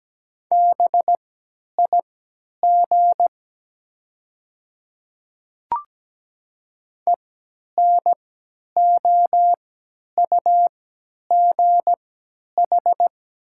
Morse code, heard as a keyed tone, about 17 wpm, spaced as for 10 wpm, 700 Hz. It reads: BIG ENOUGH
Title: 3x Character Spacing - Top 500 Words - 17wpm